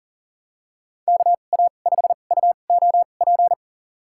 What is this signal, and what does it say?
Morse code, heard as a keyed tone, about 40 wpm, spaced as for 29 wpm, 700 Hz. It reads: KA5UOP
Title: USA Call Signs - 6 Characters - 2x Character Spacing - 40wpm